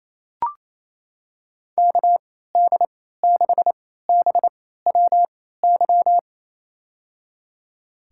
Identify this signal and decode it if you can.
Morse code, heard as a keyed tone, about 28 wpm, spaced as for 16 wpm, 700 Hz. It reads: KD6BWY
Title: USA Call Signs - 6 Characters - 3x Character Spacing - 28wpm